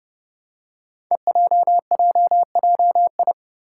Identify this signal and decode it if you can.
Morse code, heard as a keyed tone, about 28 wpm, 700 Hz. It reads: EJJJI
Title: Five Challenging Letter-Number Combinations 30wpm